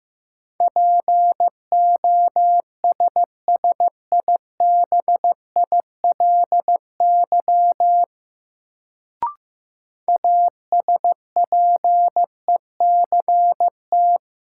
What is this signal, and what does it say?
Morse code, heard as a keyed tone, about 15 wpm, 700 Hz. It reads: POSSIBILY ASPECT
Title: Sentences from Top 1300 Words - New Word Drill - 15wpm